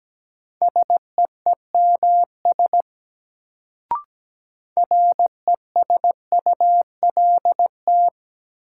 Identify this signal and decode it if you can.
Morse code, heard as a keyed tone, about 17 wpm, 700 Hz. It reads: SEEMS RESULT
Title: Sets of 3 Words - New Word Drill - 17wpm